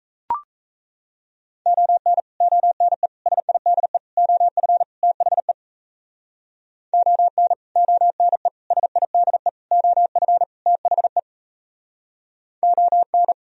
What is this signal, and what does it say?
Morse code, heard as a keyed tone, about 43 wpm, 700 Hz. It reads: ON ONE SIDE OF THE ON ONE SIDE OF THE ON